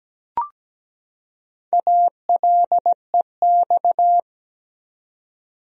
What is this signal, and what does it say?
Morse code, heard as a keyed tone, about 17 wpm, 700 Hz. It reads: ALEX
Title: USA Common Names - Max 5 Letters - 17wpm